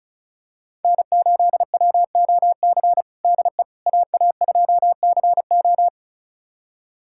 Morse code, heard as a keyed tone, about 36 wpm, 700 Hz. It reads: N8WOC DE AA2CO
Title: USA Callsign DE Callsign - 5 Characters - 35wpm